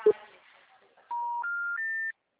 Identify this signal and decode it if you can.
Special information tone, with speech.